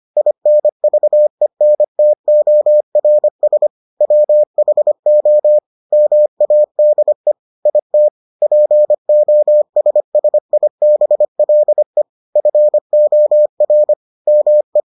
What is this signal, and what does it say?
Morse code, heard as a keyed tone, about 25 wpm, 600 Hz. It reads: INVENTORS WHO MADE IT POSSIBLE FOR ME